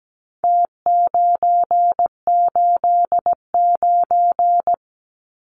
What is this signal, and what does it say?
Morse code, heard as a keyed tone, about 17 wpm, 700 Hz. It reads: T989
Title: Numbers - Contest Style - 4-Digit 17wpm